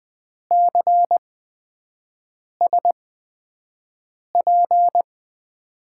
Morse code, heard as a keyed tone, about 20 wpm, spaced as for 5 wpm, 700 Hz. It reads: CSP
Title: USA Call Signs - 8x Character Spacing - 20wpm